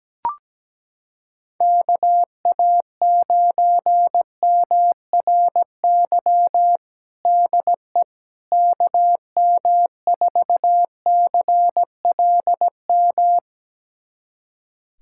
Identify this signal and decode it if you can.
Morse code, heard as a keyed tone, about 17 wpm, 700 Hz. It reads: KA9MRY DE KM4CLM
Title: USA Callsign DE Callsign - 6 Characters - 17wpm